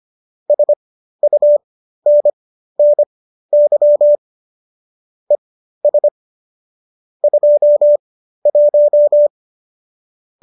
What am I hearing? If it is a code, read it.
Morse code, heard as a keyed tone, about 25 wpm, spaced as for 13 wpm, 590 Hz. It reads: SUNNY ES 21